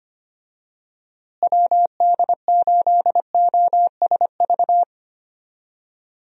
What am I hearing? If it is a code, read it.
Morse code, heard as a keyed tone, about 25 wpm, 700 Hz. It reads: WD8OSV
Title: USA Call Signs - 6 Characters - Repeated 3x - 25wpm